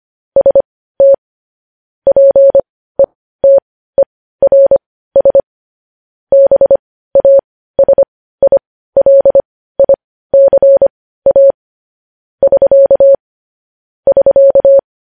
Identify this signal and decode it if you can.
Morse code, heard as a keyed tone, about 25 wpm, spaced as for 15 wpm, 550 Hz. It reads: ST PETERS BASILICA <SK> <SK>
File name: podcast / News headlines-1x-25x15